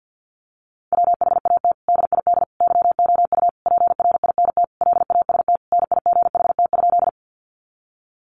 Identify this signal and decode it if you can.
Morse code, heard as a keyed tone, about 50 wpm, 700 Hz. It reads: WHAT DID YOU PRINT LAST NIGHT?
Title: Sentences from Top 1400 Words 50wpm